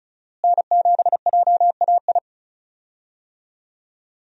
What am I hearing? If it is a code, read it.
Morse code, heard as a keyed tone, about 36 wpm, 700 Hz. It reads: N7JAI
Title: USA Call Signs - 5 Characters - Repeated 3x - 35wpm